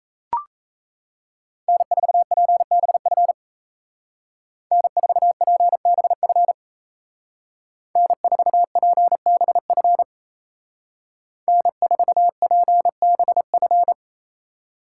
Morse code, heard as a keyed tone, about 37 wpm, 700 Hz. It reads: N4PBF N4PBF N4PBF N4PBF